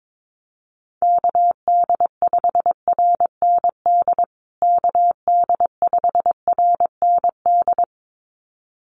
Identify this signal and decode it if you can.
Morse code, heard as a keyed tone, about 22 wpm, 700 Hz. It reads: KD5RND KD5RND
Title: USA Call Signs Repeated 2x 22wpm